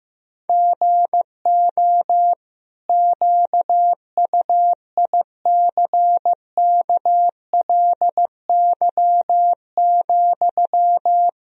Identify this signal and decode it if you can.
Morse code, heard as a keyed tone, about 15 wpm, 700 Hz. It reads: GO QUICKLY,
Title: Sentences from Top 1000 Words - Review Part 4 - 15wpm